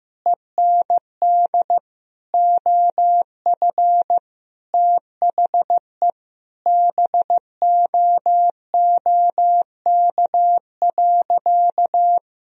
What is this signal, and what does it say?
Morse code, heard as a keyed tone, about 16 wpm, 700 Hz. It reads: END OF THE BOOK.
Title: Sentences from Top 1000 Words - Review Part 1 - 15wpm